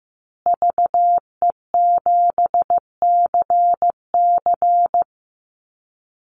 Morse code, heard as a keyed tone, about 15 wpm, 700 Hz. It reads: VE7CC
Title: Call Signs Repeated 2x - 15wpm